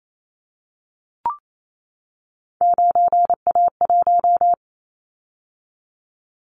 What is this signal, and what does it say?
Morse code, heard as a keyed tone, about 28 wpm, 700 Hz. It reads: 9A1
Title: Callsign Prefixes for Top 50 DXCC - 28wpm